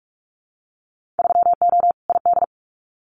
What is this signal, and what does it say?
Morse code, heard as a keyed tone, about 44 wpm, 700 Hz. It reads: 3O ID
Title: Winter Field Day Exchanges 45wpm